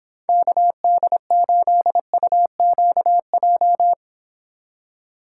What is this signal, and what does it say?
Morse code, heard as a keyed tone, about 30 wpm, 700 Hz. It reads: KD8UQJ